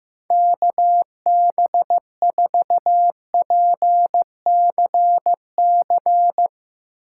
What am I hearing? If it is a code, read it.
Morse code, heard as a keyed tone, about 15 wpm, 700 Hz. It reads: KB4PCC
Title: Speed Racing - USA Call Signs 15wpm